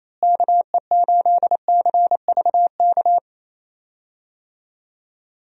Morse code, heard as a keyed tone, about 28 wpm, 700 Hz. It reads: KE8CVK
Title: USA Call Signs - 6 Characters - Repeated 2x - 28wpm